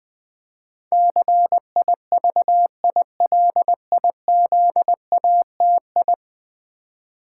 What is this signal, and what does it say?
Morse code, heard as a keyed tone, about 20 wpm, 700 Hz. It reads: CIVILIZATI
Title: Build A Word - 5 Letters At A Time - 12-Letter Words - 20wpm